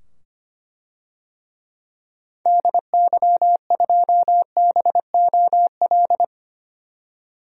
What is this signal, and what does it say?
Morse code, heard as a keyed tone, about 25 wpm, 700 Hz. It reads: DY2BOL
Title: DXCC-100 Call Signs - 6 Characters - 25wpm